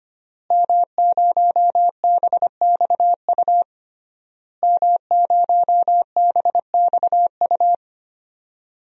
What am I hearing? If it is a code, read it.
Morse code, heard as a keyed tone, about 25 wpm, 700 Hz. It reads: M0BXU M0BXU
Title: DXCC-100 Call Signs - 5 Characters - Repeated 3x - 25wpm